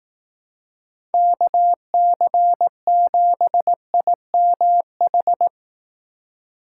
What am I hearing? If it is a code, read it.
Morse code, heard as a keyed tone, about 18 wpm, 700 Hz. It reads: KC7IMH
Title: Speed Racing - USA Call Signs 15wpm